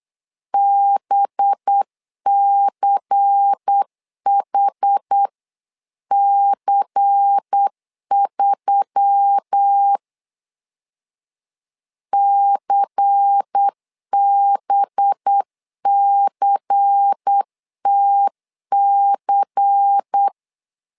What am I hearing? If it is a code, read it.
Morse code, heard as a keyed tone, about 8 wpm, 800 Hz. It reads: BCHC3 CBCTC
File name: LEZIONI / 5 LBVC / 5 LBVC 8wpm 2935